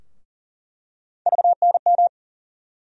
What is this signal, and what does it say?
Morse code, heard as a keyed tone, about 40 wpm, 700 Hz. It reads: VNM
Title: USA Call Sign Postfixes - 3 Letters 40wpm